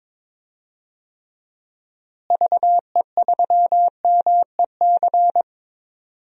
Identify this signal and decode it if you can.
Morse code, heard as a keyed tone, about 22 wpm, 700 Hz. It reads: VE3MEC
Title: DXCC-100 Call Signs - 6 Characters - Repeated 3x - 22wpm